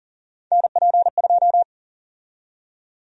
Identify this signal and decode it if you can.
Morse code, heard as a keyed tone, about 40 wpm, 700 Hz. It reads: NP2
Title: USA Call Sign Prefixes - Repeated 2x - 40wpm